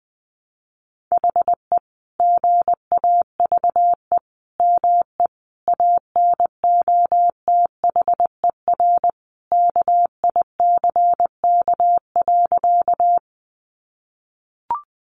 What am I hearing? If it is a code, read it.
Morse code, heard as a keyed tone, about 20 wpm, 700 Hz. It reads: HE GAVE ME ANOTHER KICK.
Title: Sentences from Top 1300 Words 20wpm